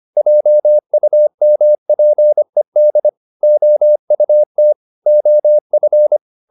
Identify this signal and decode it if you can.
Morse code, heard as a keyed tone, about 25 wpm, 600 Hz. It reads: JUMPED OUT OF